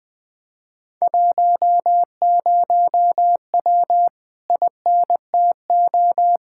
Morse code, heard as a keyed tone, about 20 wpm, 700 Hz. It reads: 10W INTO
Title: Speed-Racing Common QSO Phrases - 20wpm